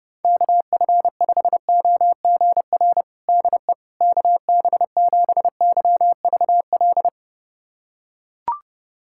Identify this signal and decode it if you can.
Morse code, heard as a keyed tone, about 30 wpm, 700 Hz. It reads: KF5OGR DE KB7YVL
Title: USA Callsign DE Callsign - 6 Characters - 30wpm